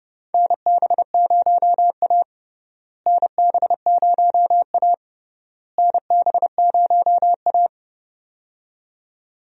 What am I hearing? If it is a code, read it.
Morse code, heard as a keyed tone, about 30 wpm, 700 Hz. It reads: NB0A NB0A NB0A
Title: USA Call Signs - 4 Characters - Repeated 3x - 30wpm